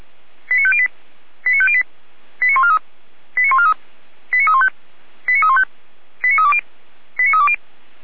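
Five-tone selective call (ZVEI), heard as quick runs of five tones, tones of about 70 ms. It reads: 89508 89508 89143 89143 89315 89315 89320 89320
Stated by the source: ZVEI five-tone selective call 89280 received by SDR.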